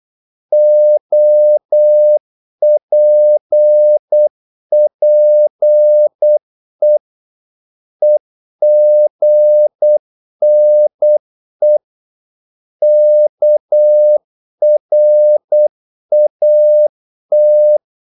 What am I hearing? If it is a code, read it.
Morse code, heard as a keyed tone, about 8 wpm, 600 Hz. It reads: OPPE EGNE KRAT
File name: Korte ord DK 8wpm